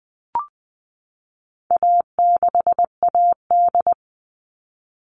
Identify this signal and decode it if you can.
Morse code, heard as a keyed tone, about 20 wpm, 700 Hz. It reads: A6AD
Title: All Cut Numbers - 4-Digit 20wpm